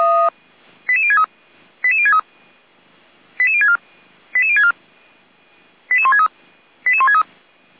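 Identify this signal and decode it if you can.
Five-tone selective call (ZVEI), heard as quick runs of five tones, tones of about 70 ms. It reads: 88063 88063 88064 88064 88163 88163
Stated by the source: ZVEI five-tone call received on SDR.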